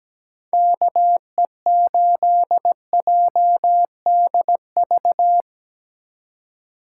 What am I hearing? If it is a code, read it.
Morse code, heard as a keyed tone, about 19 wpm, 700 Hz. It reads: KE8JDV
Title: Speed Racing - USA 2x3 Call Signs 17wpm